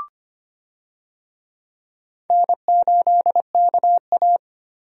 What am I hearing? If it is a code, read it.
Morse code, heard as a keyed tone, about 25 wpm, 700 Hz. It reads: N8KA
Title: USA Call Signs - 4 Characters - 25wpm